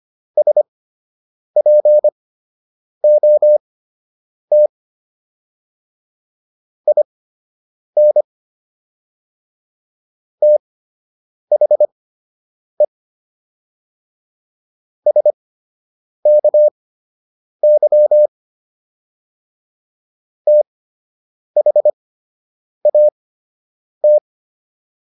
Morse code, heard as a keyed tone, about 25 wpm, spaced as for 8 wpm, 600 Hz. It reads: SPOT IN THE SKY THAT